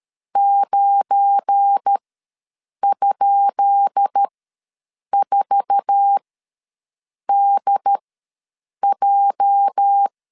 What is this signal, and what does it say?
Morse code, heard as a keyed tone, about 13 wpm, spaced as for 7 wpm, 800 Hz. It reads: 9?4DJ
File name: LEZIONI / 10 SIGNS / 10B INT ws 12wpm 3527